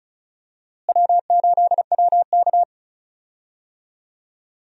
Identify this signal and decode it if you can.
Morse code, heard as a keyed tone, about 34 wpm, 700 Hz. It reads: W8WK
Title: USA Call Signs - 4 Characters - Repeated 2x - 35wpm